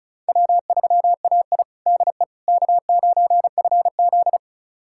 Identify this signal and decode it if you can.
Morse code, heard as a keyed tone, about 36 wpm, 700 Hz. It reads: W3AI DE K9FZ
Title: USA Callsign DE Callsign - 4 Characters - 35wpm